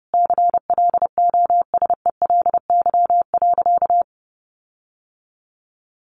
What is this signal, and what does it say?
Morse code, heard as a keyed tone, about 30 wpm, 700 Hz. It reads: CLOSELY.
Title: Sentences from Top 1400 Words 30wpm